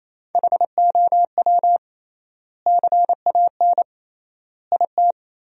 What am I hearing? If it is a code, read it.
Morse code, heard as a keyed tone, about 28 wpm, 700 Hz. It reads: HOW CAN IT
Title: 3x Word-Spacing - Sentences from Top 100 Words - 28wpm